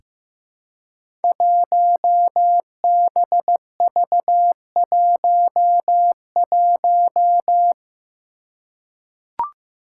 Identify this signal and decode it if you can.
Morse code, heard as a keyed tone, about 15 wpm, 700 Hz. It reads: 1BV11
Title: Code Groups - 5 Letter:Numbers - 15wpm